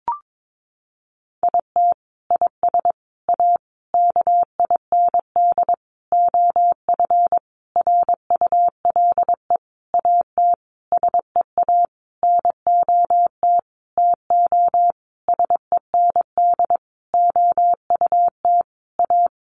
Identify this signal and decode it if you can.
Morse code, heard as a keyed tone, about 22 wpm, 700 Hz. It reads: IT IS A KIND OF RULE AT SEA NOT TO SEND OUT A
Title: Sentences from Top 1500 Words 22wpm